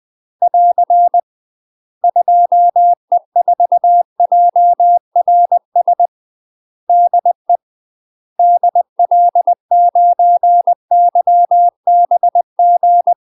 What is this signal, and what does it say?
Morse code, heard as a keyed tone, about 20 wpm, 700 Hz. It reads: <AR> 2E4JRS DE DL9YBG
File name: UK MASTER QSOs TRAINING / 20WPM / UK MASTER QSOs 0008